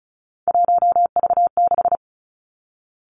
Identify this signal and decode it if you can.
Morse code, heard as a keyed tone, about 36 wpm, 700 Hz. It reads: 1V6
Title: Three Challenging Letter-Number Combinations 35wpm